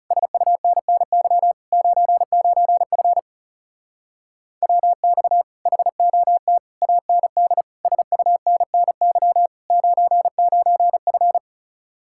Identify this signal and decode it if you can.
Morse code, heard as a keyed tone, about 40 wpm, 700 Hz. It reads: SUNNY 99F WX HOT AND SUNNY 99F